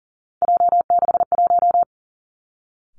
Morse code, heard as a keyed tone, about 38 wpm, 700 Hz. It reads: J61
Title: Three Challenging Letter-Number Combinations 40wpm